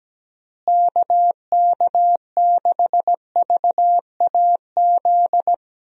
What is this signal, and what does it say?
Morse code, heard as a keyed tone, about 17 wpm, 700 Hz. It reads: KK6VAZ
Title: USA Call Signs - 6 Characters - Repeated 3x - 17wpm